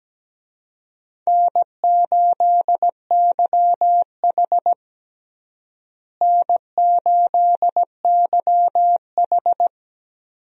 Morse code, heard as a keyed tone, about 17 wpm, 700 Hz. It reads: N8YH N8YH
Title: USA Call Signs - 4 Characters - Repeated 2x - 17wpm